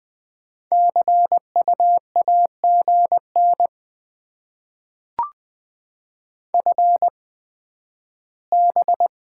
Morse code, heard as a keyed tone, about 20 wpm, spaced as for 12 wpm, 700 Hz. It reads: CU AGN FB
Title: Top 20 QSO Elements - 8x Character Spacing - 20wpm